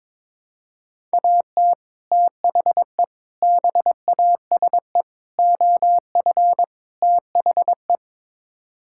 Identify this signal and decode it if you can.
Morse code, heard as a keyed tone, about 22 wpm, 700 Hz. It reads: AT THE BASE OF THE
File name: practice / 22wpm / Build A Phrase - 5-Word Phrases - 22wpm